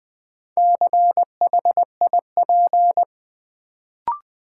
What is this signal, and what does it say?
Morse code, heard as a keyed tone, about 20 wpm, 700 Hz. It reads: CHIP
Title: Build A Word - 2 Letters At A Time - 4-Letter Words - 20wpm